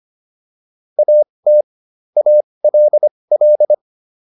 Morse code, heard as a keyed tone, about 25 wpm, spaced as for 20 wpm, 600 Hz. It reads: AT ALL